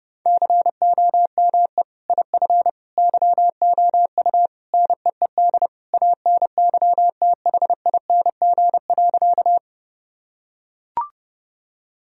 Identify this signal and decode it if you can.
Morse code, heard as a keyed tone, about 30 wpm, 700 Hz. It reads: COME IF YOU NEED ANYTHING.
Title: Sentences from Top 1000 Words - Review Part 1 - 30wpm